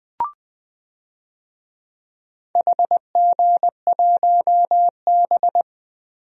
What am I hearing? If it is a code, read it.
Morse code, heard as a keyed tone, about 21 wpm, 700 Hz. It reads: HG1B
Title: DXCC-100 Call Signs - 4 Characters - Repeated 3x - 20wpm